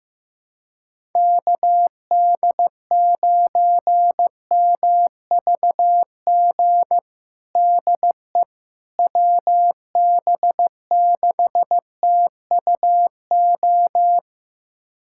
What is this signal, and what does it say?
Morse code, heard as a keyed tone, about 16 wpm, 700 Hz. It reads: KD9MVG DE WB6TUO
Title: USA Callsign DE Callsign 15wpm